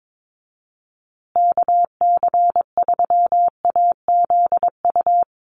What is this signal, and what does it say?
Morse code, heard as a keyed tone, about 22 wpm, 700 Hz. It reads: KC3AZU
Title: USA Call Signs Repeated 2x 22wpm